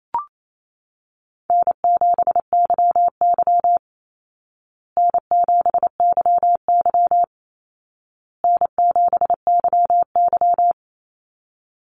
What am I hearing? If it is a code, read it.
Morse code, heard as a keyed tone, about 28 wpm, 700 Hz. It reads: N7YY N7YY N7YY
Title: Call Signs Repeated 2x - 28wpm